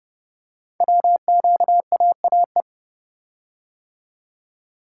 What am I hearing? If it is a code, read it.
Morse code, heard as a keyed tone, about 30 wpm, 700 Hz. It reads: WQAAE
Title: Code Groups - 5 Letter:Numbers - 30wpm